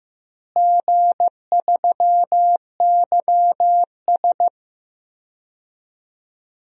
Morse code, heard as a keyed tone, about 15 wpm, 700 Hz. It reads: G3YS
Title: DXCC-100 Call Signs - 4 Characters - Repeated 2x - 15wpm